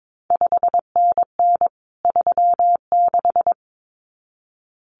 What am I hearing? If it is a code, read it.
Morse code, heard as a keyed tone, about 22 wpm, 700 Hz. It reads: 5NN 36
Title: CQ WW DX Contest Exchange Encore - 22wpm